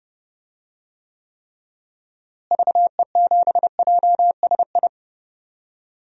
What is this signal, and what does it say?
Morse code, heard as a keyed tone, about 30 wpm, 700 Hz. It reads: VE7JSI
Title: DXCC-100 Call Signs - 6 Characters - Repeated 3x - 30wpm